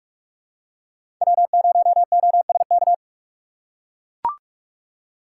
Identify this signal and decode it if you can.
Morse code, heard as a keyed tone, about 44 wpm, 700 Hz. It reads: W0OSK
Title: USA Call Signs - 3x Character Spacing - 45wpm